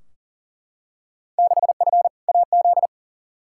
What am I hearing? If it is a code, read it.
Morse code, heard as a keyed tone, about 40 wpm, 700 Hz. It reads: 6F AZ
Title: Field Day Exchanges Encore - 40wpm